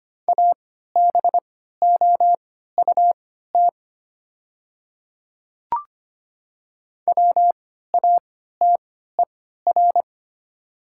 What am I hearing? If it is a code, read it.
Morse code, heard as a keyed tone, about 25 wpm, spaced as for 14 wpm, 700 Hz. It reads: ABOUT WATER
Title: 3x Character Spacing - Top 100 Words - 25wpm